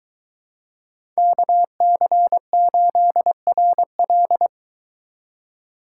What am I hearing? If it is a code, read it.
Morse code, heard as a keyed tone, about 23 wpm, 700 Hz. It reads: KC8RL